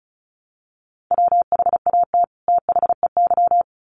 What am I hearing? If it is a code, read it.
Morse code, heard as a keyed tone, about 34 wpm, 700 Hz. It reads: WHAT THEY
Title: Sets of 2 Words 35wpm